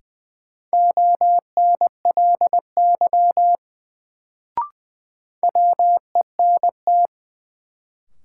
Morse code, heard as a keyed tone, about 21 wpm, 700 Hz. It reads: ONLY WENT
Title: Sets of 2 Words - New Word Drill - 20wpm